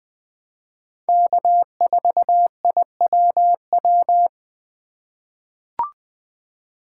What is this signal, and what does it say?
Morse code, heard as a keyed tone, about 20 wpm, 700 Hz. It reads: K4IWW
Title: USA Call Signs - 5 Characters - 2x Character Spacing - 20wpm